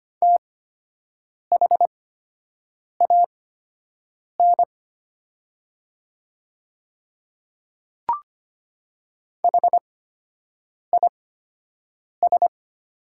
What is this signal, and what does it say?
Morse code, heard as a keyed tone, about 25 wpm, spaced as for 7 wpm, 700 Hz. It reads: THAN HIS
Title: 8x Character Spacing - Top 100 Words - 25wpm